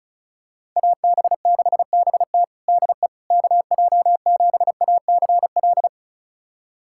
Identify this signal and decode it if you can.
Morse code, heard as a keyed tone, about 36 wpm, 700 Hz. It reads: AB6BT DE KJ7ACL
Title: USA Callsign DE Callsign 35wpm